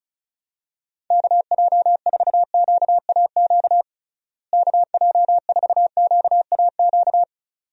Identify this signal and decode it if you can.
Morse code, heard as a keyed tone, about 34 wpm, 700 Hz. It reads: KJ4QAQ KJ4QAQ
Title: USA Call Signs - 6 Characters - Repeated 2x - 35wpm